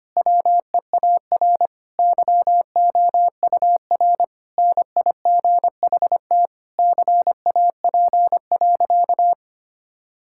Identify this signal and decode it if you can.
Morse code, heard as a keyed tone, about 25 wpm, 700 Hz. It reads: WEAR YOUR NIGHT CAP.
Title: Sentences from Top 1600 Words 25wpm